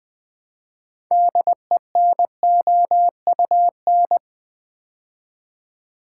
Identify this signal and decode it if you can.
Morse code, heard as a keyed tone, about 20 wpm, 700 Hz. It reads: DENOUN
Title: Build A Word - 2 Letters At A Time - 8-Letter Words - 20wpm